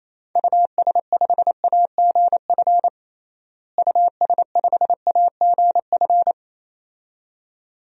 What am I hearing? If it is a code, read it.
Morse code, heard as a keyed tone, about 28 wpm, 700 Hz. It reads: US5AGF US5AGF
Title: DXCC-100 Call Signs - 6 Characters - Repeated 2x - 28wpm